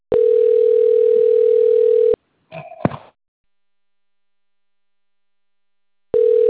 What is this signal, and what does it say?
Ringback tone (North America), 440 + 480 Hz.
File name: out- - - - - .84848